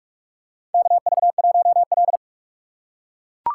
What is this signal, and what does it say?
Morse code, heard as a keyed tone, about 42 wpm, 700 Hz. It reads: KV1L
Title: USA Call Signs - 4 Characters - 6x Character Spacing - 45wpm